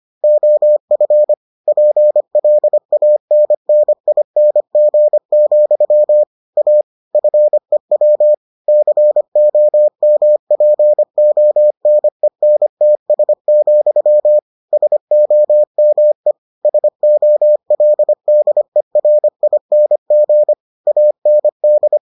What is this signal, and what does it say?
Morse code, heard as a keyed tone, about 25 wpm, 600 Hz. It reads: OF PLANNING, A FEW COMPONENTS, SOME SOLDERING AND